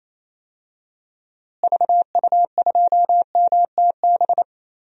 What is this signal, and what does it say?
Morse code, heard as a keyed tone, about 28 wpm, 700 Hz. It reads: VU2MTB
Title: DXCC-100 Call Signs - 6 Characters - Repeated 3x - 28wpm